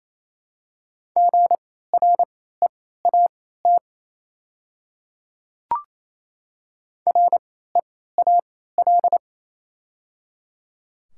Morse code, heard as a keyed tone, about 28 wpm, spaced as for 16 wpm, 700 Hz. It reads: GREAT REAL
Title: 3x Character Spacing - Top 500 Words - 28wpm